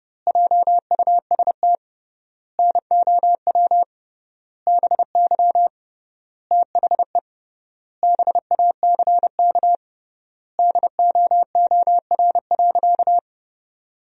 Morse code, heard as a keyed tone, about 32 wpm, 700 Hz. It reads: JUST NOW BY THE BACK DOOR.
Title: 3x Word-Spacing - Sentences from Top 300 Words - 30wpm